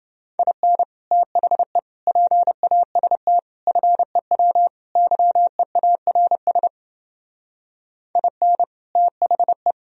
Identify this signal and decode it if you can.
Morse code, heard as a keyed tone, about 29 wpm, 700 Hz. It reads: IN THE PAST FEW YEARS IN THE